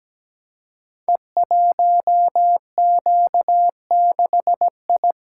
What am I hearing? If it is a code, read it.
Morse code, heard as a keyed tone, about 17 wpm, 700 Hz. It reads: E1Q6I
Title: Code Groups - 5 Letter:Numbers - 17wpm